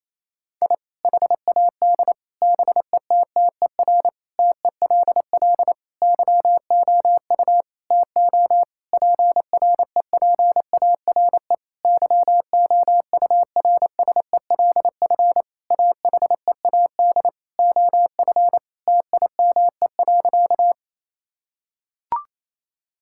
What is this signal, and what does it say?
Morse code, heard as a keyed tone, about 28 wpm, 700 Hz. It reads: I HAD BETTER TELL YOU TO PREPARE YOURSELF AHEAD OF TIME.
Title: Sentences from Top 1000 Words - Review Part 6 - 28wpm